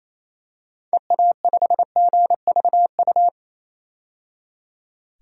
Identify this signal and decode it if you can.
Morse code, heard as a keyed tone, about 28 wpm, 700 Hz. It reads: EA5GVU
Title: DXCC-100 Call Signs - 6 Characters - 28wpm